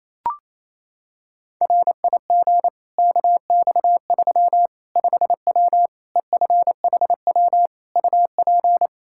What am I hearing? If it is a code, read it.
Morse code, heard as a keyed tone, about 28 wpm, 700 Hz. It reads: RIG KX3 5W EFHW UP
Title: Practice QSO - Exchange 2 - Info - 28wpm